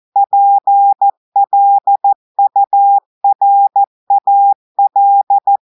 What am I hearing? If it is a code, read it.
Morse code, heard as a keyed tone, about 14 wpm, 800 Hz. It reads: PLURAL
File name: audios / portugais / portugais texte en clair 1 vitesse 14 mots minutes